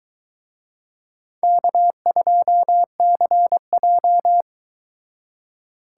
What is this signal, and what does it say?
Morse code, heard as a keyed tone, about 23 wpm, 700 Hz. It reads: K2CJ